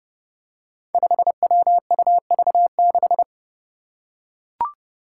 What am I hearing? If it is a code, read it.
Morse code, heard as a keyed tone, about 30 wpm, 700 Hz. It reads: 5WUV6
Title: Five Challenging Letter-Number Combinations 30wpm